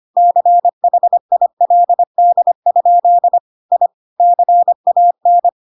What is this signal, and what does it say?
Morse code, heard as a keyed tone, about 25 wpm, 700 Hz. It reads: CHILD? I CAN
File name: Pinocchio / Pinocchio ch1 25 25 ik1hge